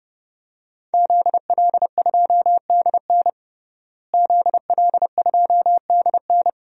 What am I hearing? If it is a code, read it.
Morse code, heard as a keyed tone, about 30 wpm, 700 Hz. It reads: ZL2DN ZL2DN
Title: DXCC-100 Call Signs - 5 Characters - Repeated 2x - 30wpm